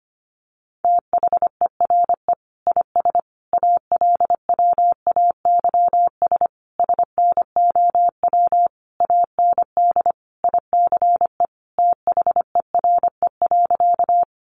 Morse code, heard as a keyed tone, about 25 wpm, 700 Hz. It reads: THERE IS ALWAYS SNOW AND ICE THERE.
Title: Sentences from Top 1000 Words 25wpm